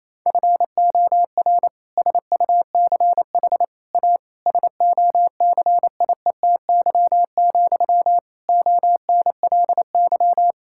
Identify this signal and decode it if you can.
Morse code, heard as a keyed tone, about 28 wpm, 700 Hz. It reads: FOR SUCH A SOCIETY, ONLY
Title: Sentences from Top 1000 Words - Review Part 6 - 28wpm